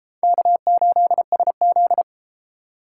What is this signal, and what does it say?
Morse code, heard as a keyed tone, about 32 wpm, 700 Hz. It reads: K8SZ